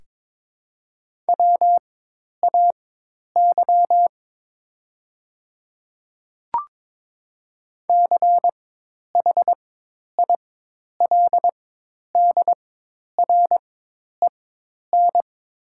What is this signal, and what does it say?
Morse code, heard as a keyed tone, about 22 wpm, spaced as for 10 wpm, 700 Hz. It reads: WAY CHILDREN
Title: 4x Character Spacing - Top 300 Words - 22wpm